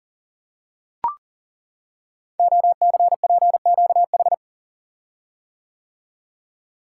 Morse code, heard as a keyed tone, about 40 wpm, 700 Hz. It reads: OCPQH
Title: Code Groups - 5 Letter:Numbers - 40wpm